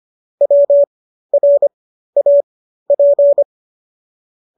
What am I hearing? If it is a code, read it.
Morse code, heard as a keyed tone, about 25 wpm, spaced as for 13 wpm, 570 Hz. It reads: WRAP